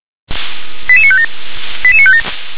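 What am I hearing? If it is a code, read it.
Five-tone selective call (ZVEI), heard as quick runs of five tones, tones of about 70 ms. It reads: 88957 88957
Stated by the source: ZVEI five-tone selective call received on SDR.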